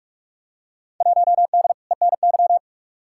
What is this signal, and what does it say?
Morse code, heard as a keyed tone, about 47 wpm, 700 Hz. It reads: 1D ENY
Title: Field Day Exchanges - 45wpm